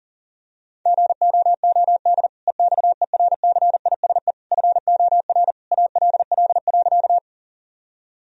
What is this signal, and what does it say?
Morse code, heard as a keyed tone, about 40 wpm, 700 Hz. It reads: GOOD EXERCISE FOR ALL.
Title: Sentences from Top 1000 Words - Review Part 4 - 40wpm